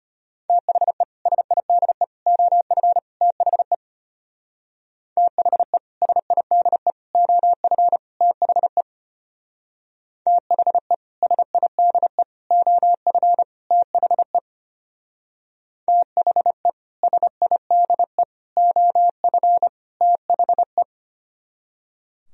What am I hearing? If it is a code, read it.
Morse code, heard as a keyed tone, about 32 wpm, 700 Hz. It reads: THE SIDE OF THE THE SIDE OF THE THE SIDE OF THE THE SIDE OF THE